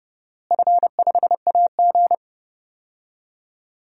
Morse code, heard as a keyed tone, about 31 wpm, 700 Hz. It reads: F5AG
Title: DXCC-100 Call Signs - 4 Characters - Repeated 3x - 30wpm